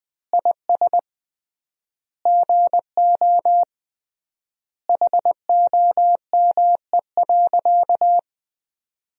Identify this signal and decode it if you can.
Morse code, heard as a keyed tone, about 20 wpm, 700 Hz. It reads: IS GO HOME.
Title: 3x Word-Spacing - Sentences from Top 200 Words - 20wpm